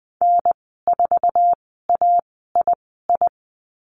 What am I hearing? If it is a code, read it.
Morse code, heard as a keyed tone, about 20 wpm, spaced as for 15 wpm, 700 Hz. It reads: N4AII
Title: 055 Lesson 10g - Letter r - Call Signs 20wpm